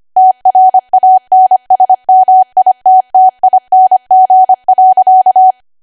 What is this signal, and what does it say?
Morse code, heard as a keyed tone, about 25 wpm, 750 Hz. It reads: TRANSMITTING.
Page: 25 WPM Code Archive